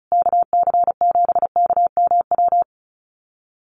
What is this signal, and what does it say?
Morse code, heard as a keyed tone, about 34 wpm, 700 Hz. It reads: KC7KMW
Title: USA Call Signs Repeated 3x 35wpm